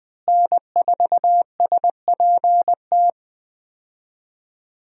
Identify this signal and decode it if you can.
Morse code, heard as a keyed tone, about 20 wpm, 700 Hz. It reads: N4SPT
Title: USA Call Signs - 5 Characters - Repeated 2x - 20wpm